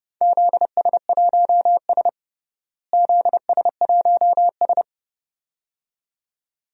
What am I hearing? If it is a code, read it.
Morse code, heard as a keyed tone, about 32 wpm, 700 Hz. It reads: ZS1S ZS1S
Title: DXCC-100 Call Signs - 4 Characters - Repeated 2x - 30wpm